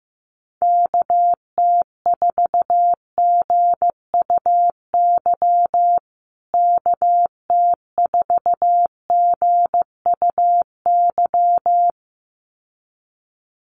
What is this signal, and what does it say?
Morse code, heard as a keyed tone, about 15 wpm, 700 Hz. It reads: KT4GUY KT4GUY
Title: USA Call Signs Repeated 2x 15wpm